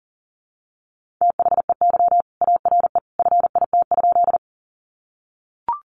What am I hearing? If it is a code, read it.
Morse code, heard as a keyed tone, about 40 wpm, 700 Hz. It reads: THEY ARE FIT?
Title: Sentences from Top 800 Words 40wpm